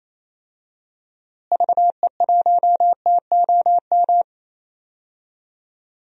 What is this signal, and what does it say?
Morse code, heard as a keyed tone, about 28 wpm, 700 Hz. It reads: VE1TOM
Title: DXCC-100 Call Signs - 6 Characters - Repeated 3x - 28wpm